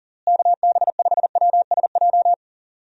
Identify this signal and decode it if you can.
Morse code, heard as a keyed tone, about 40 wpm, 700 Hz. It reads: KB5WSJ
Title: USA Call Signs - 6 Characters - Repeated 3x - 40wpm